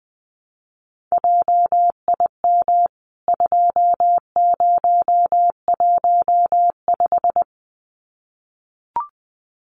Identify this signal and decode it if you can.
Morse code, heard as a keyed tone, about 20 wpm, 700 Hz. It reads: JIM 2015
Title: CWT Exchange 20wpm